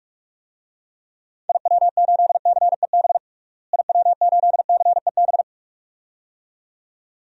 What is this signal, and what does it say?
Morse code, heard as a keyed tone, about 46 wpm, 700 Hz. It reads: IW8CEB IW8CEB
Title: DXCC-100 Call Signs - 6 Characters - Repeated 2x - 45wpm